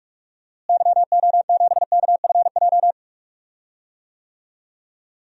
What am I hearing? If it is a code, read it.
Morse code, heard as a keyed tone, about 45 wpm, 700 Hz. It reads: YO7KFJ
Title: DXCC-100 Call Signs - 6 Characters - Repeated 2x - 45wpm